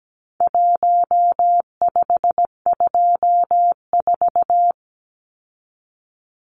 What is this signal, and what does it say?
Morse code, heard as a keyed tone, about 17 wpm, 700 Hz. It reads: 1524
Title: Numbers - 4-Digit 17wpm